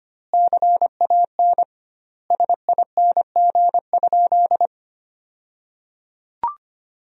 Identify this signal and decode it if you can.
Morse code, heard as a keyed tone, about 25 wpm, 700 Hz. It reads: CAN SING?
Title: 2x Word-Spacing - Sentences from Top 400 Words - 25wpm